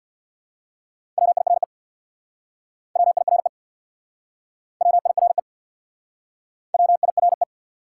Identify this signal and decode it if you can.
Morse code, heard as a keyed tone, about 69 wpm, 700 Hz. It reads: WIRE WIRE WIRE WIRE